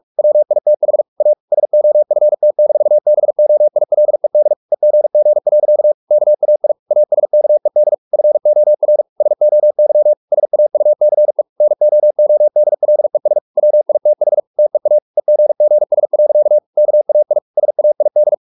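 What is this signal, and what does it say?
Morse code, heard as a keyed tone, about 44 wpm, 600 Hz. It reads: WITH A SOFT-BOILED EGG. KAI ASKED FOR SOY SAUCE NOODLES WITH TEA EGGS. KAI SAID
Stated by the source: Morse practice at 45 wpm, the novel Never by Ken Follett in English, from CW med Gnister.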